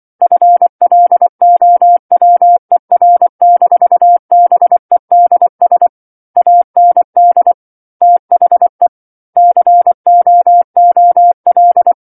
Morse code, heard as a keyed tone, about 24 wpm, 700 Hz. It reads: FLOWER-BEDS AND THE COOL
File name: 07-Chapter-07 24 WPM